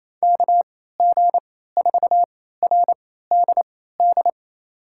Morse code, heard as a keyed tone, about 28 wpm, spaced as for 16 wpm, 700 Hz. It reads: KG4RDD
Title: USA Call Signs - 6 Characters - 3x Character Spacing - 28wpm